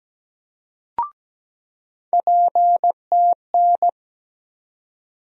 Morse code, heard as a keyed tone, about 17 wpm, 700 Hz. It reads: PTN
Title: USA Call Sign Postfixes - 3 Letters 17wpm